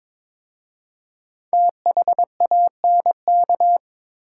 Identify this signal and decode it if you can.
Morse code, heard as a keyed tone, about 22 wpm, 700 Hz. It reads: THANK
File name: 5 Letter Words - First 3 Letters Given - 22wpm